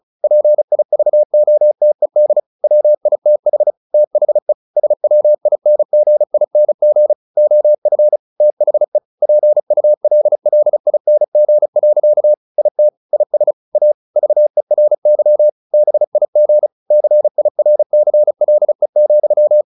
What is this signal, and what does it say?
Morse code, heard as a keyed tone, about 34 wpm, 600 Hz. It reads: PIVOTED WITH THE SWINGING OF THE PULLING. IT IS A VERY BIG CIRCLE,
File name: the old man and the sea 2 35wpm